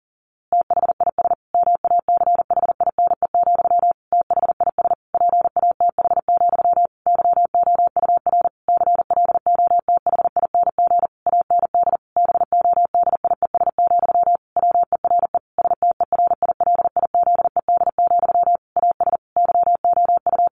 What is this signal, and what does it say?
Morse code, heard as a keyed tone, about 37 wpm, 700 Hz. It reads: THIS MACHINE, THIS PATH, YOUR CLOTHING AND BODIES, WERE STERILIZED, AS YOU
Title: A Sound Of Thunder 40wpm